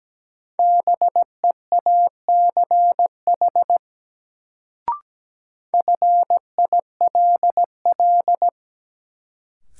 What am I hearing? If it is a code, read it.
Morse code, heard as a keyed tone, about 17 wpm, 700 Hz. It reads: BEACH FILL
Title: Sentences from Top 1500 Words - New Word Drill Alt - 17wpm